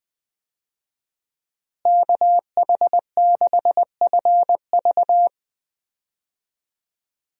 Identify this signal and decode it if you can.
Morse code, heard as a keyed tone, about 20 wpm, 700 Hz. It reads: KH6FV
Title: USA Call Signs - 5 Characters - Repeated 2x - 20wpm